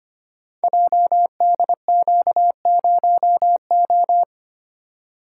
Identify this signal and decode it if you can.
Morse code, heard as a keyed tone, about 25 wpm, 700 Hz. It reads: JDQ0O
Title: Code Groups - 5 Letter:Numbers - 25wpm